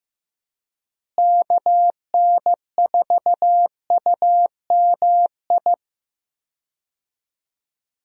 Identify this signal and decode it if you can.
Morse code, heard as a keyed tone, about 15 wpm, 700 Hz. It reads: KN4UMI
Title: USA Call Signs - 6 Characters - Repeated 3x - 15wpm